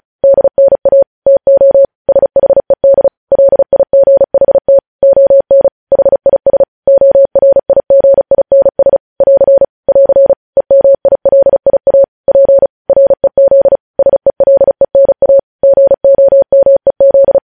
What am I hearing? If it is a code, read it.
Morse code, heard as a keyed tone, about 36 wpm, 550 Hz. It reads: DNA TO SHED LIGHT ON HIS ORIGINS <AR> <AR> EMILIA P REZ SELENA GOMEZ